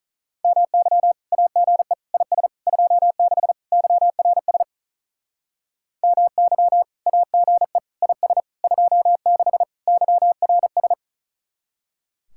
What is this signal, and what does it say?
Morse code, heard as a keyed tone, about 44 wpm, 700 Hz. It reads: MY AGE IS 26 YRS MY AGE IS 26 YRS